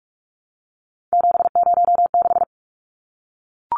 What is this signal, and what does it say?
Morse code, heard as a keyed tone, about 44 wpm, 700 Hz. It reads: 706
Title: Numbers - 3-Digit 45wpm